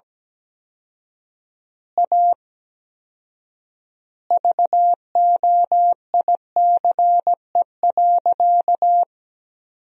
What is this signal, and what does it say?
Morse code, heard as a keyed tone, about 17 wpm, 700 Hz. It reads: A VOICE.
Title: 4x Word-Spacing - Sentences from Top 500 Words - 17wpm